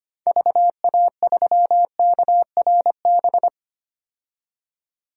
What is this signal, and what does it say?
Morse code, heard as a keyed tone, about 25 wpm, 700 Hz. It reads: VA3KRB
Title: DXCC-100 Call Signs - 6 Characters - 25wpm